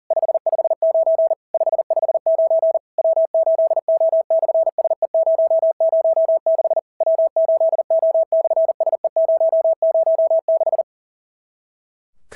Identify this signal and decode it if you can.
Morse code, heard as a keyed tone, about 40 wpm, 650 Hz. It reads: 559 559 W8O/SE006 W8O/SE006
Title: SOTA QSO Practice 40wpm